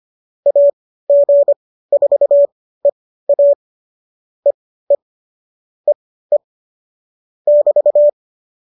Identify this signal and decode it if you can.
Morse code, heard as a keyed tone, about 25 wpm, spaced as for 15 wpm, 570 Hz. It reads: AG4EA EE EE <BT>